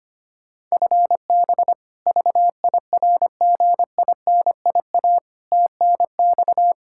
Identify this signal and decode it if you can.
Morse code, heard as a keyed tone, about 25 wpm, 700 Hz. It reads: FB VIRGINIA TNX
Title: K1USN SST QSO Practice 25wpm